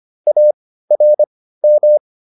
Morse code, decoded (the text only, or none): ARM